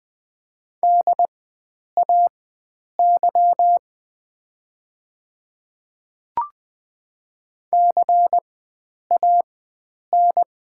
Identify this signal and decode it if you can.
Morse code, heard as a keyed tone, about 20 wpm, spaced as for 9 wpm, 700 Hz. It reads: DAY CAN
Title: 4x Character Spacing - Top 100 Words - 20wpm